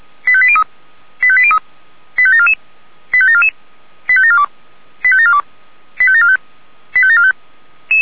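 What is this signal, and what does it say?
Five-tone selective call (ZVEI), heard as quick runs of five tones, tones of about 70 ms. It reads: 85803 85803 85744 85744 85742 85742 85745 85745
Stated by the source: ZVEI five-tone selective call 85726, received by SDR.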